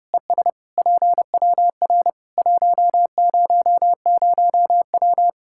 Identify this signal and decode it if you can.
Morse code, heard as a keyed tone, about 30 wpm, 700 Hz. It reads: ES PWR 100W
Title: Practice QSO - Exchange 2 - Info - 30wpm